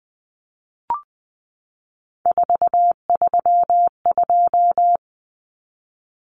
Morse code, heard as a keyed tone, about 20 wpm, 700 Hz. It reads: 432
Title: Common Cut Numbers - 3-Digit 20wpm